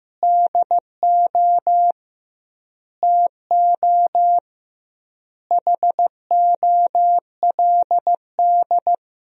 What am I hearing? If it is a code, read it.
Morse code, heard as a keyed tone, about 15 wpm, 700 Hz. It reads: DO TO HOLD
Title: 2x Word-Spacing - Sentences from Top 400 Words - 15wpm